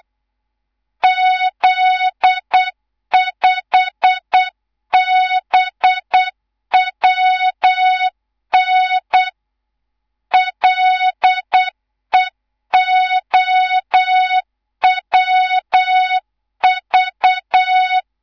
Morse code, heard as a keyed tone, about 8 wpm, 3700 Hz, with noise. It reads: Z5BWN LEOWV
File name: Morze-HA2MN/hang/11 GYAKORLAT